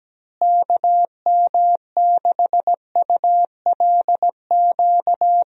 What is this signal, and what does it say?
Morse code, heard as a keyed tone, about 17 wpm, 700 Hz. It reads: KM6ULQ
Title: USA Call Signs - 6 Characters - Repeated 3x - 17wpm